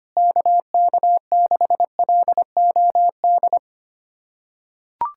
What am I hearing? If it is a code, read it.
Morse code, heard as a keyed tone, about 25 wpm, 700 Hz. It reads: KK6LOD
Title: USA Callsign DE Callsign - 6 Characters - 25wpm